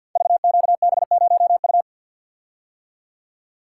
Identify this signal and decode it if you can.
Morse code, heard as a keyed tone, about 47 wpm, 700 Hz. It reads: VQB0U
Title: Code Groups - 5 Letter:Numbers - 50wpm